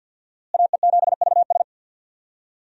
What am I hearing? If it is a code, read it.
Morse code, heard as a keyed tone, about 45 wpm, 700 Hz. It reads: AE7VS
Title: USA Call Signs - 5 Characters - Repeated 2x - 50wpm